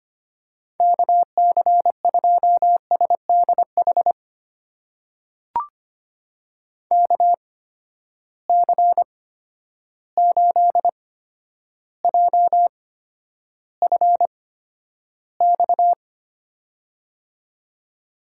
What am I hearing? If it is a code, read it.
Morse code, heard as a keyed tone, about 25 wpm, spaced as for 11 wpm, 700 Hz. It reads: KC2SDH KC8JFX
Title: USA Call Signs - 8x Character Spacing - 25wpm